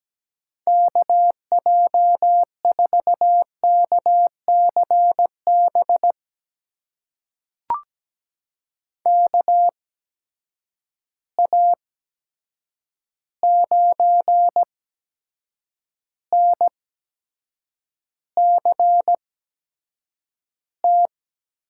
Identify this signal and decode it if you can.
Morse code, heard as a keyed tone, about 17 wpm, spaced as for 7 wpm, 700 Hz. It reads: KJ4KCB KA9NCT
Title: USA Call Signs - 6 Characters - 8x Character Spacing - 17wpm